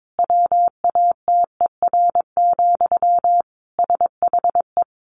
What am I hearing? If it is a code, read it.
Morse code, heard as a keyed tone, about 22 wpm, 700 Hz. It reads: WATER, SHE
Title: Sentences from Top 800 Words 22wpm